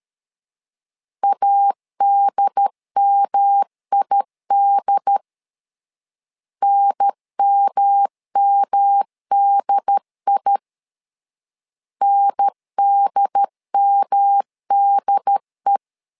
Morse code, heard as a keyed tone, about 13 wpm, 800 Hz. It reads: ADMID NMMDI NDMDE
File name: LEZIONI / 3 MWRO / 3 M 12wpm 1541